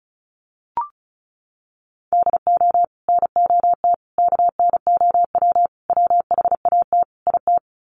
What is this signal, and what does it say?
Morse code, heard as a keyed tone, about 34 wpm, 700 Hz. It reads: DO NOT KNOW WHAT IT
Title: Sets of 5 Words 35wpm